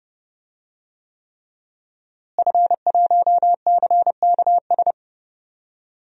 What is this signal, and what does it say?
Morse code, heard as a keyed tone, about 30 wpm, 700 Hz. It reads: F1CKS
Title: DXCC-100 Call Signs - 5 Characters - Repeated 3x - 30wpm